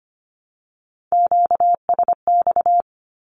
Morse code, heard as a keyed tone, about 25 wpm, 700 Hz. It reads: QSX
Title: Q-Codes Repeated 3x 25wpm